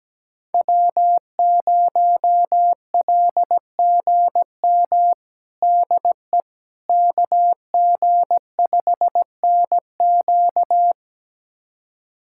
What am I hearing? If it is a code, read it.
Morse code, heard as a keyed tone, about 17 wpm, 700 Hz. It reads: W0LGM DE KG5NQ
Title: USA Callsign DE Callsign - 5 Characters - 17wpm